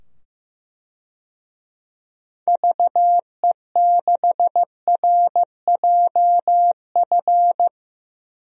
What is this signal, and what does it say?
Morse code, heard as a keyed tone, about 15 wpm, 700 Hz. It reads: VE6RJF
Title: DXCC-100 Call Signs - 6 Characters - 15wpm